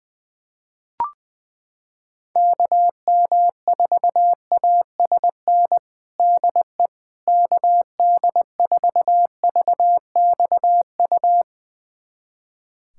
Morse code, heard as a keyed tone, about 20 wpm, 700 Hz. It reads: KM4ASN DE KD4VXU
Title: USA Callsign DE Callsign - 6 Characters - 20wpm